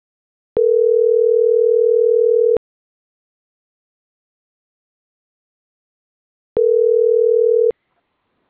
Ringback tone (North America), 440 + 480 Hz.